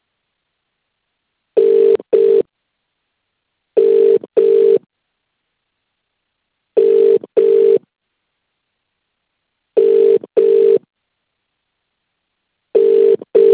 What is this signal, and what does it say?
Ringback tone (UK), 400 + 450 Hz.